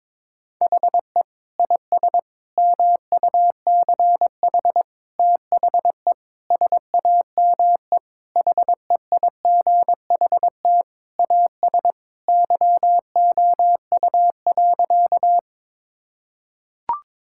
Morse code, heard as a keyed tone, about 22 wpm, 700 Hz. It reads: HE IS MUCH THE SAME HEIGHT AS YOU.
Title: Sentences from Top 1600 Words 22wpm